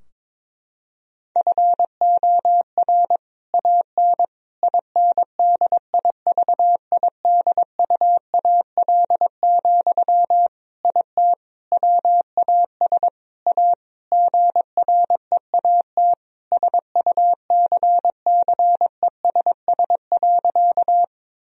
Morse code, heard as a keyed tone, about 22 wpm, 700 Hz. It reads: FOR AN INDIVIDUAL, IT WAS A GREAT SUCCESS.
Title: Sentences from Top 1000 Words - Review Part 4 - 22wpm